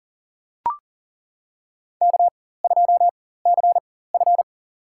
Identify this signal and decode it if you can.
Morse code, heard as a keyed tone, about 40 wpm, spaced as for 19 wpm, 700 Hz. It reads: K2CF
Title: USA Call Signs - 4 Characters - 4x Character Spacing - 40wpm